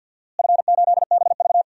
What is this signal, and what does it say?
Morse code, heard as a keyed tone, about 45 wpm, 700 Hz. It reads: F8BV
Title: DXCC-100 Call Signs - 4 Characters - Repeated 3x - 50wpm